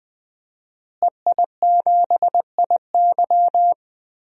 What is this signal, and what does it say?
Morse code, heard as a keyed tone, about 20 wpm, 700 Hz. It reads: EI7IY
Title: DXCC-100 Call Signs - 5 Characters - Repeated 3x - 20wpm